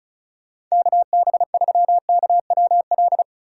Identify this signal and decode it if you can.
Morse code, heard as a keyed tone, about 34 wpm, 700 Hz. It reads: KB3KWL
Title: USA Call Signs - 2x Character Spacing - 35wpm